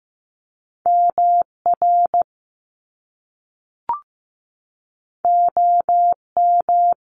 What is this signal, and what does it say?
Morse code, heard as a keyed tone, about 15 wpm, 700 Hz. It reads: MR OM
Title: Top Two Letter Combinations in Words 15wpm